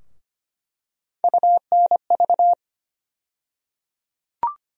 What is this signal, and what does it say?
Morse code, heard as a keyed tone, about 25 wpm, 700 Hz. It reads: UNV
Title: USA Call Sign Postfixes - 3 Letters 25wpm